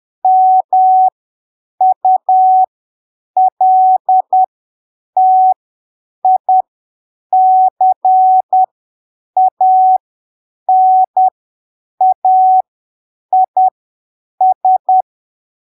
Morse code, heard as a keyed tone, about 10 wpm, spaced as for 7 wpm, 750 Hz. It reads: MULTICANAIS
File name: arquivos / texto / texto 13 10ppm